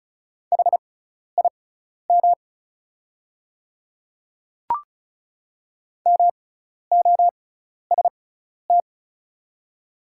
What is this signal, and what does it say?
Morse code, heard as a keyed tone, about 36 wpm, spaced as for 12 wpm, 700 Hz. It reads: HIM MOST
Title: 6x Character Spacing - Top 100 Words - 35wpm